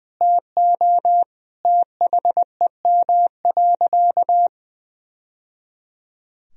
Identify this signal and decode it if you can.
Morse code, heard as a keyed tone, about 21 wpm, 700 Hz. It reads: TO THEM.
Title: Sentences from Top 1000 Words - Review Part 3 - 20wpm